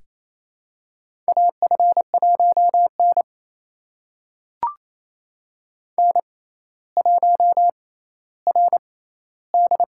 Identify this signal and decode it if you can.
Morse code, heard as a keyed tone, about 27 wpm, spaced as for 14 wpm, 700 Hz. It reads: AF1N N1RD
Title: USA Call Signs - 4 Characters - 6x Character Spacing - 28wpm